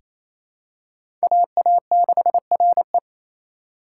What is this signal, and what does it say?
Morse code, heard as a keyed tone, about 27 wpm, 700 Hz. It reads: AA6RE
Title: K1USN SST QSO Practice - 28wpm